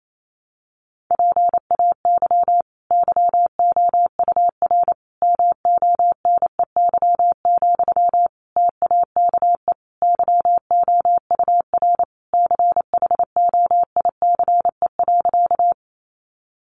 Morse code, heard as a keyed tone, about 28 wpm, 700 Hz. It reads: PAY YOUR MONEY, TAKE YOUR CHOICE.
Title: Sentences from Top 900 Words 28wpm